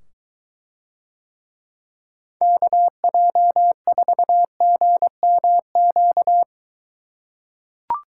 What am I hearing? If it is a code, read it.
Morse code, heard as a keyed tone, about 23 wpm, 700 Hz. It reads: KJ4GMQ